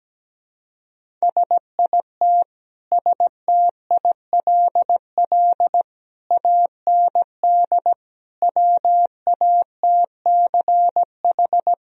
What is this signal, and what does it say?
Morse code, heard as a keyed tone, about 17 wpm, 700 Hz. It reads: SIT STILL AND WATCH
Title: Sentences from Top 1000 Words - Review Part 1 - 17wpm